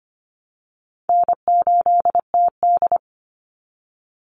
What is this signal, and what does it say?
Morse code, heard as a keyed tone, about 25 wpm, 700 Hz. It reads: N8TD
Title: Call Signs - 25wpm